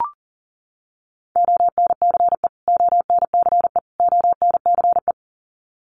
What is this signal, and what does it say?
Morse code, heard as a keyed tone, about 40 wpm, 700 Hz. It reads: ONCE ONCE ONCE
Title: Top 500 Words Repeated 40wpm